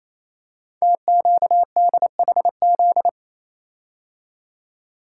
Morse code, heard as a keyed tone, about 29 wpm, 700 Hz. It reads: TQDHZ
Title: Code Groups - 5 Letter:Numbers - 28wpm